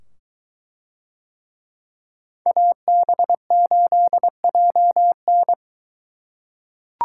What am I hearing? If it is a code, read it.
Morse code, heard as a keyed tone, about 23 wpm, 700 Hz. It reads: AB8JN